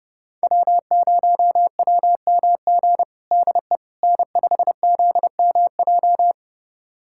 Morse code, heard as a keyed tone, about 30 wpm, 700 Hz. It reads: W0WMG DE N5ZMJ
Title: USA Callsign DE Callsign - 5 Characters - 30wpm